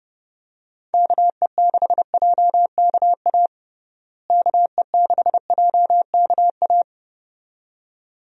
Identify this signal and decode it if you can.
Morse code, heard as a keyed tone, about 30 wpm, 700 Hz. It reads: KE6JKA KE6JKA
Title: USA Call Signs - 6 Characters - Repeated 2x - 30wpm